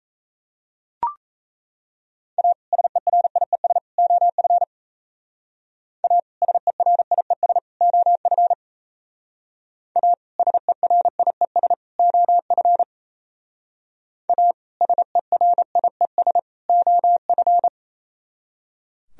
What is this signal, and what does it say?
Morse code, heard as a keyed tone, about 34 wpm, 700 Hz. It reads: A SERIES OF A SERIES OF A SERIES OF A SERIES OF
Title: Speed-Racing - Sets of 3 Words 28wpm